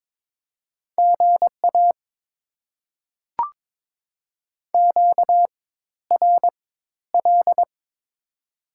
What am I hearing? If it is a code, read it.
Morse code, heard as a keyed tone, about 22 wpm, spaced as for 13 wpm, 690 Hz. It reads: GA QRL
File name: Top 20 QSO Elements - 4x Character Spacing - 22wpm